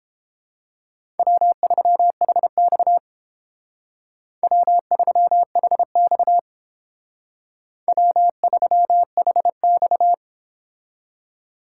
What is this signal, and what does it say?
Morse code, heard as a keyed tone, about 30 wpm, 700 Hz. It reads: W3HX W3HX W3HX